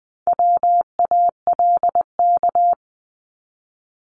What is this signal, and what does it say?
Morse code, heard as a keyed tone, about 20 wpm, 700 Hz. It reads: WALK
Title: Top 300 Words 20wpm